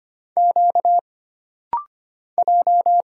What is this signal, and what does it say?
Morse code, heard as a keyed tone, about 25 wpm, 700 Hz. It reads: Q J
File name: Single Character (Letter:Number:Punctuation:Prosign) - Rapid-Fire - 25wpm